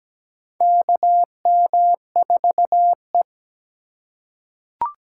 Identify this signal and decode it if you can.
Morse code, heard as a keyed tone, about 17 wpm, 700 Hz. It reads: KM4E
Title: USA Call Signs - 4 Characters - 2x Character Spacing - 17wpm